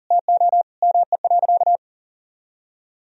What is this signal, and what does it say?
Morse code, heard as a keyed tone, about 40 wpm, 700 Hz. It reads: TO ME.
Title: Sentences from Top 500 Words and Abbreviations 40wpm